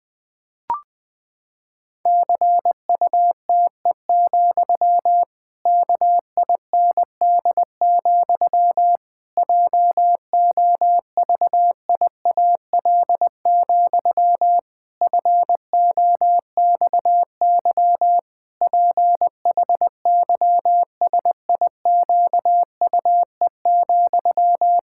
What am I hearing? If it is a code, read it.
Morse code, heard as a keyed tone, about 20 wpm, 700 Hz. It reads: CUTE, KIND, JOVIAL, FOXY PHYSIQUE,
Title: Pangram Challenge v2 - 20wpm